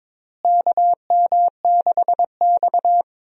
Morse code, heard as a keyed tone, about 22 wpm, 700 Hz. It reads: KM6X
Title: USA Call Signs - 4 Characters - Repeated 3x - 22wpm